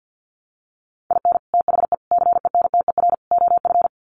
Morse code, heard as a keyed tone, about 50 wpm, 700 Hz. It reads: IN THE CENTER OF
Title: Sets of 4 Words 50wpm